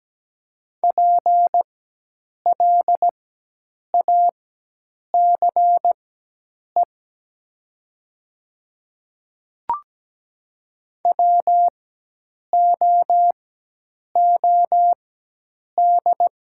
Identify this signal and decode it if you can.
Morse code, heard as a keyed tone, about 17 wpm, spaced as for 8 wpm, 700 Hz. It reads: PLACE WOOD
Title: 4x Character Spacing - Top 300 Words - 17wpm